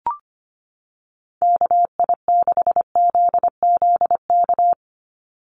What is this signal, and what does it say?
Morse code, heard as a keyed tone, about 25 wpm, 700 Hz. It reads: KI6ZZK
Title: USA Call Signs 25wpm